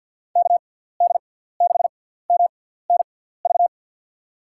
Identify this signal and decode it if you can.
Morse code, heard as a keyed tone, about 53 wpm, spaced as for 17 wpm, 700 Hz. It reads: KD6MNV
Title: USA Call Signs - 6 Characters - 6x Character Spacing - 50wpm